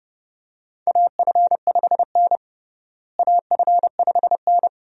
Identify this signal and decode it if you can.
Morse code, heard as a keyed tone, about 32 wpm, 700 Hz. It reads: AF5N AF5N
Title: USA Call Signs - 4 Characters - Repeated 2x - 30wpm